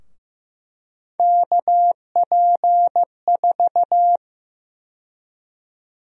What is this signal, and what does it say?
Morse code, heard as a keyed tone, about 15 wpm, 700 Hz. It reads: KP4
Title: USA Call Sign Prefixes - 15wpm